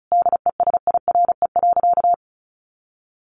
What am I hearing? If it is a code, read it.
Morse code, heard as a keyed tone, about 34 wpm, 700 Hz. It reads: DESIRE.
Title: Sentences from Top 1100 Words 35wpm